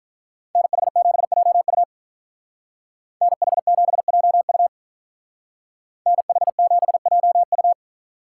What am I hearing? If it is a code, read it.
Morse code, heard as a keyed tone, about 45 wpm, 700 Hz. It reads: NH7JU NH7JU NH7JU